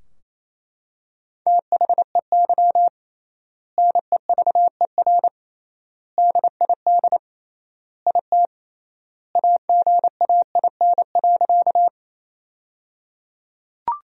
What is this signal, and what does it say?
Morse code, heard as a keyed tone, about 28 wpm, 700 Hz. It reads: THEY NEVER DID IT AGAIN.
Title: 3x Word-Spacing - Sentences from Top 300 Words - 28wpm